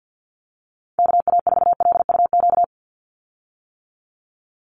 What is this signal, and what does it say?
Morse code, heard as a keyed tone, about 50 wpm, 700 Hz. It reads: KA4LUQ
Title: USA Call Signs 50wpm